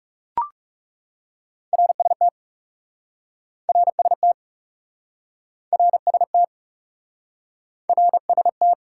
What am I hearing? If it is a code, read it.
Morse code, heard as a keyed tone, about 40 wpm, 700 Hz. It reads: RST RST RST RST